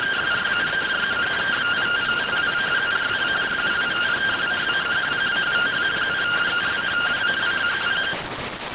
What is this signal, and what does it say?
Two-tone teleprinter signal, about 170 Hz shift.